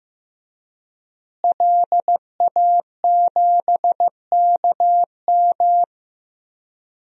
Morse code, heard as a keyed tone, about 15 wpm, 700 Hz. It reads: LA7KM
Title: DXCC-100 Call Signs - 5 Characters - Repeated 2x - 15wpm